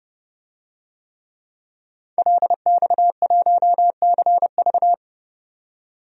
Morse code, heard as a keyed tone, about 30 wpm, 700 Hz. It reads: LX1CV
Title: DXCC-100 Call Signs - 5 Characters - 30wpm